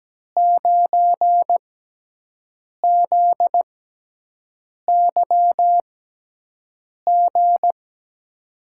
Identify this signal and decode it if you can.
Morse code, heard as a keyed tone, about 17 wpm, spaced as for 6 wpm, 700 Hz. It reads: 9ZYG
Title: USA Call Signs - 6 Characters - 6x Character Spacing - 17wpm